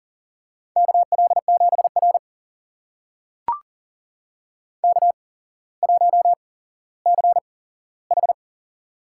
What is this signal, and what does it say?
Morse code, heard as a keyed tone, about 38 wpm, spaced as for 17 wpm, 700 Hz. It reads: KL7R K1CH
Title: USA Call Signs - 4 Characters - 8x Character Spacing - 40wpm